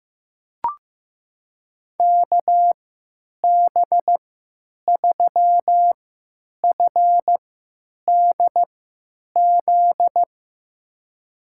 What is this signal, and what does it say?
Morse code, heard as a keyed tone, about 15 wpm, spaced as for 9 wpm, 700 Hz. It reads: KB3FDZ
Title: USA Call Signs - 6 Characters - 3x Character Spacing - 15wpm